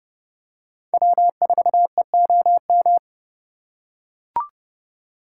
Morse code, heard as a keyed tone, about 29 wpm, 700 Hz. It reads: W4EOM